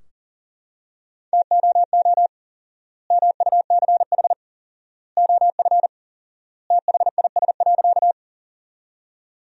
Morse code, heard as a keyed tone, about 40 wpm, 700 Hz. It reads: TOO MUCH OF THIS.
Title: 4x Word-Spacing - Sentences from Top 200 Words - 40wpm